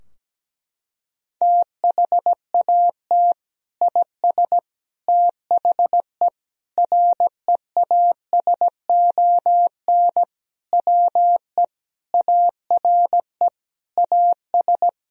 Morse code, heard as a keyed tone, about 17 wpm, 700 Hz. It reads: THAT IS THE REASON WE ARE AS
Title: Sentences from Top 1000 Words - Review Part 4 - 17wpm